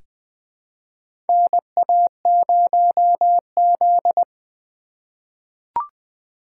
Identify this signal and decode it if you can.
Morse code, heard as a keyed tone, about 20 wpm, 700 Hz. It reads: NA0Z
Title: USA Call Signs - 4 Characters - 4x Character Spacing - 20wpm